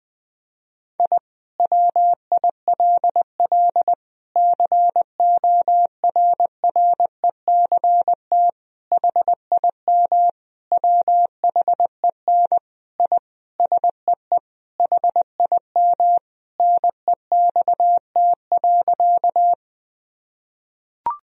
Morse code, heard as a keyed tone, about 20 wpm, 700 Hz. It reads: I WILL CORRECT HIM WHEN I SEE HIM NEXT.
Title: Sentences from Top 1000 Words - Review Part 2 - 20wpm